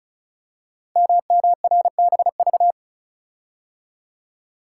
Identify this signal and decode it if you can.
Morse code, heard as a keyed tone, about 36 wpm, 700 Hz. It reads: MMRBV
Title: Code Groups - 5 Letters - 35wpm